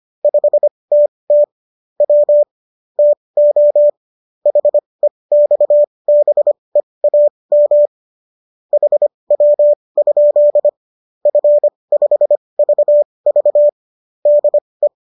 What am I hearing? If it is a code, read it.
Morse code, heard as a keyed tone, about 25 wpm, spaced as for 20 wpm, 590 Hz. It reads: 5TT W TO HEXBEAM HW? F5VV DE